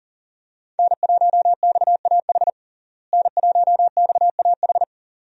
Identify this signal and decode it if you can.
Morse code, heard as a keyed tone, about 40 wpm, 700 Hz. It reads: N1XAH N1XAH
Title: USA Call Signs - 5 Characters - Repeated 2x - 40wpm